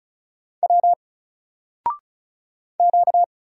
Morse code, heard as a keyed tone, about 36 wpm, 700 Hz. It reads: W Q
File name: Single-Letter-Number-35wpm